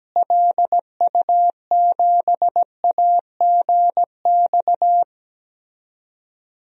Morse code, heard as a keyed tone, about 17 wpm, 700 Hz. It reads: LU7AGX
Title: DXCC-100 Call Signs - 6 Characters - Repeated 3x - 17wpm